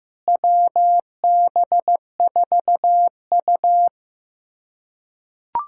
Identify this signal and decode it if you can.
Morse code, heard as a keyed tone, about 15 wpm, 700 Hz. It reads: WB4U
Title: USA Callsign DE Callsign 15wpm